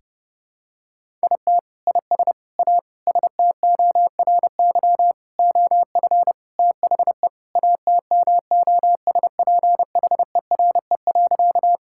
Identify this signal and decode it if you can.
Morse code, heard as a keyed tone, about 30 wpm, 700 Hz. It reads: IT IS A STORY OF THE ATMOSPHERE.
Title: Sentences from Top 1600 Words 30wpm